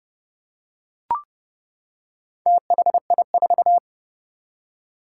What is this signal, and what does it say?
Morse code, heard as a keyed tone, about 28 wpm, 700 Hz. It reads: THI4
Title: Four Challenging Letter-Number Combinations 30wpm